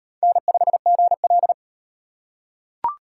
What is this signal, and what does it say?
Morse code, heard as a keyed tone, about 37 wpm, 700 Hz. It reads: N5GL